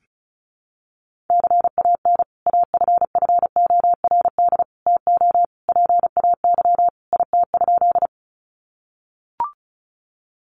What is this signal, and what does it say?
Morse code, heard as a keyed tone, about 34 wpm, 700 Hz. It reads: CAN AFFORD TO PAY IT?
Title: Sentences from Top 1100 Words 35wpm